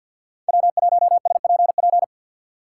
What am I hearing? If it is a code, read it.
Morse code, heard as a keyed tone, about 52 wpm, 700 Hz. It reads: W1SPP
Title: USA Call Signs - 5 Characters - 2x Character Spacing - 50wpm